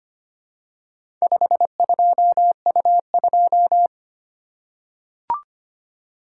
Morse code, heard as a keyed tone, about 25 wpm, 700 Hz. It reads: 52U2
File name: practice / 25wpm / Four Letter-Number Combinations - 25wpm